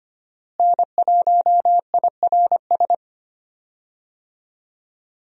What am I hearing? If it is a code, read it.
Morse code, heard as a keyed tone, about 25 wpm, 700 Hz. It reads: N1IRS
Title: USA Call Signs - 5 Characters - Repeated 3x - 25wpm